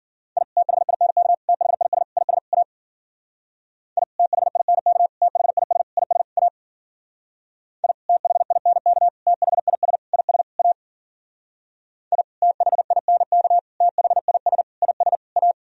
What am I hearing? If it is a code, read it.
Morse code, heard as a keyed tone, about 52 wpm, 700 Hz. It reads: I THINK THIS IS A I THINK THIS IS A I THINK THIS IS A I THINK THIS IS A